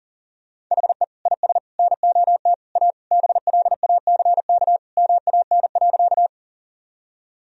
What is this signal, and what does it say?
Morse code, heard as a keyed tone, about 40 wpm, 700 Hz. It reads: HE IS NOT A BLACK MAN.
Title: Sentences from Top 1000 Words - Review Part 2 - 40wpm